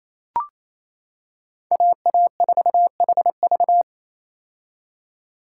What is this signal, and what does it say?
Morse code, heard as a keyed tone, about 28 wpm, 700 Hz. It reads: AA4HV
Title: K1USN SST QSO Practice - 28wpm